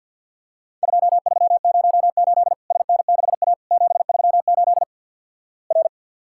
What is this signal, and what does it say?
Morse code, heard as a keyed tone, about 50 wpm, 700 Hz. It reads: 2308 SN6A 738 R
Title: Worked All Europe DX Contest - QTC Exchange with Repeats - 50wpm